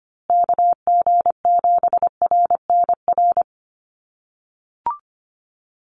Morse code, heard as a keyed tone, about 25 wpm, 700 Hz. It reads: KG7RNR
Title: USA Call Signs Repeated 2x 25wpm